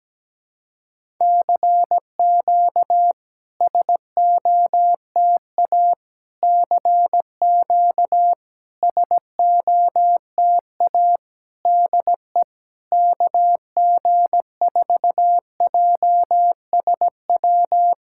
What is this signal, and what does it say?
Morse code, heard as a keyed tone, about 17 wpm, 700 Hz. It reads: CQ SOTA CQ SOTA DE KG4JSW
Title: SOTA QSO Practice 17wpm